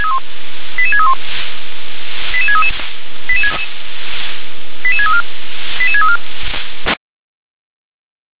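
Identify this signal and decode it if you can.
Five-tone selective call (ZVEI), heard as quick runs of five tones, tones of about 70 ms. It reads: 88631 88633 88634 88634
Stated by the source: Five-tone ZVEI selective call received by SDR.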